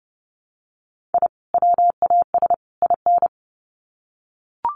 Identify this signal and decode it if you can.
Morse code, heard as a keyed tone, about 30 wpm, 700 Hz. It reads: I WAS IN
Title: Sets of 3 Words 30wpm